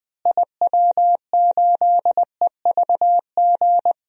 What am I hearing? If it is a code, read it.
Morse code, heard as a keyed tone, about 20 wpm, 700 Hz. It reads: IW8EVG
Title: DXCC-100 Call Signs - 6 Characters - Repeated 2x - 20wpm